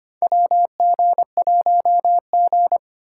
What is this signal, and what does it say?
Morse code, heard as a keyed tone, about 25 wpm, 700 Hz. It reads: WG1G
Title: USA Call Signs - 2x Character Spacing - 25wpm